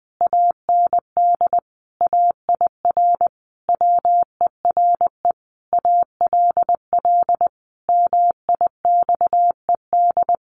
Morse code, heard as a keyed tone, about 20 wpm, 700 Hz. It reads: AND AIR WERE ALL MIXED
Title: Sentences from Top 1500 Words 20wpm